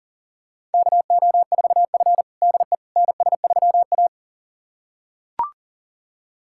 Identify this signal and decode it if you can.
Morse code, heard as a keyed tone, about 40 wpm, 700 Hz. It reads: KO4F DE NS3A
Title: USA Callsign DE Callsign - 4 Characters - 40wpm